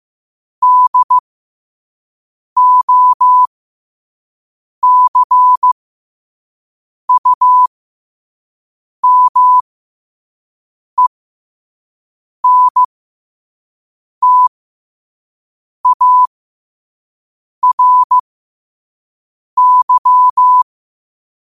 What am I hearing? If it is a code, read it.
Morse code, heard as a keyed tone, about 15 wpm, spaced as for 5 wpm, 1000 Hz. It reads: DOCUMENTARY
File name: qotdfiles / 5wpm